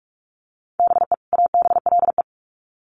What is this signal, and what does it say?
Morse code, heard as a keyed tone, about 46 wpm, 700 Hz. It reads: BE ABLE
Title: Sets of 2 Words 45wpm